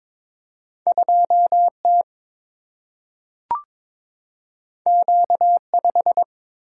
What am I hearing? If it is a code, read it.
Morse code, heard as a keyed tone, about 22 wpm, 700 Hz. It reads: 2T Q5
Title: Single-Letter Single-Number 22wpm